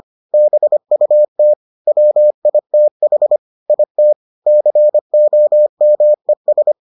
Morse code, heard as a keyed tone, about 25 wpm, 600 Hz. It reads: BUT WITH IT COMES